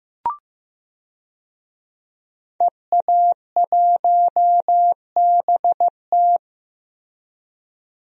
Morse code, heard as a keyed tone, about 15 wpm, 700 Hz. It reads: EA1BT
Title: DXCC-100 Call Signs - 5 Characters - Repeated 2x - 15wpm